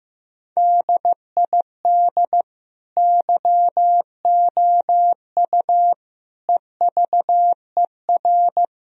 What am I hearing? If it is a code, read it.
Morse code, heard as a keyed tone, about 15 wpm, 700 Hz. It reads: DID YOU EVER
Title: Sentences from Top 1000 Words - Review Part 1 - 15wpm